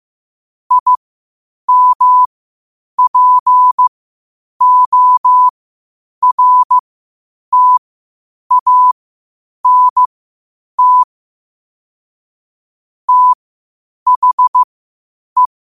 Morse code, heard as a keyed tone, about 15 wpm, spaced as for 8 wpm, 1000 Hz. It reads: IMPORTANT THE
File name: qotdfiles / 8wpm